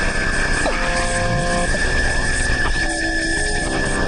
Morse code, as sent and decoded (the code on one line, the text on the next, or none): none